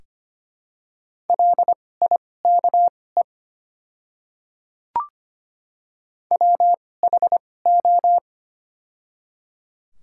Morse code, heard as a keyed tone, about 25 wpm, spaced as for 18 wpm, 700 Hz. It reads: LIKE WHO
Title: 2x Character Spacing - Top 100 Words - 25wpm